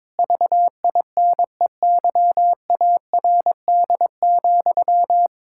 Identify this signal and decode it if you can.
Morse code, heard as a keyed tone, about 22 wpm, 700 Hz. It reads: VINEYARD,
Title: Pangram Challenge v2 - 22wpm